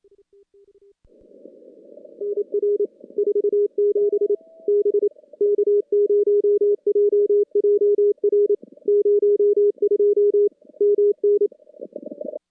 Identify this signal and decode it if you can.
Morse code, heard as a keyed tone, about 27 wpm, 410 Hz. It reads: NR 47 B K0JJR 02 MN
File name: contests / CW-SS-k5tr- -2300